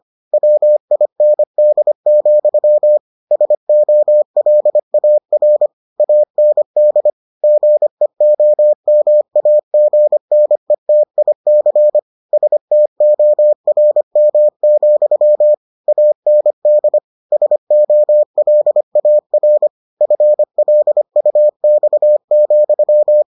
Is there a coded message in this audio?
Yes: Morse code from 0.3 to 23.2 s, heard as a keyed tone, about 25 wpm, 600 Hz. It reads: WIND, SOLAR AND GEOMAGNETIC STORM, AND SOLAR FLUX,